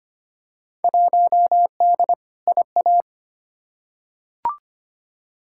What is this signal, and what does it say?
Morse code, heard as a keyed tone, about 25 wpm, 700 Hz. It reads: 1D IA
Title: Field Day Exchanges - 25wpm